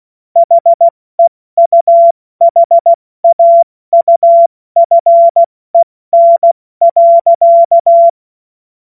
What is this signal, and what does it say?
Morse code, heard as a keyed tone, about 16 wpm, 680 Hz. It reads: HEUHAUFEN.
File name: Morsekurs / Witze / Witz 07. Dezember 2021, 70 F80